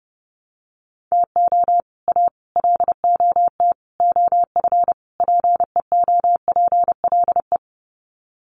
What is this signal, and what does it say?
Morse code, heard as a keyed tone, about 30 wpm, 700 Hz. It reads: TO A LOT OF PEOPLE
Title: Sets of 5 Words Encore 30wpm